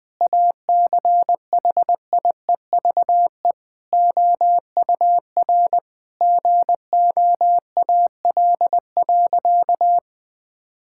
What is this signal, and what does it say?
Morse code, heard as a keyed tone, about 20 wpm, 700 Hz. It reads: ACHIEVE OUR GOAL.
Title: Sentences from Top 1000 Words - Review Part 6 - 20wpm